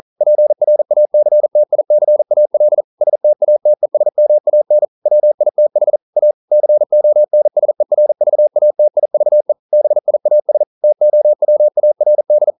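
Morse code, heard as a keyed tone, about 42 wpm, 600 Hz. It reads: PRACTICAL STATESMAN WITH A CONSERVATIVE BIAS TOWARD
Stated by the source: Morse code practice at 41 wpm, from CW med Gnister.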